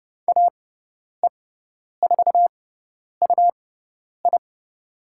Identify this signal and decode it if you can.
Morse code, heard as a keyed tone, about 30 wpm, spaced as for 10 wpm, 700 Hz. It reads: AE4UI
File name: Lesson 18g - Letter m - Call Signs 30wpm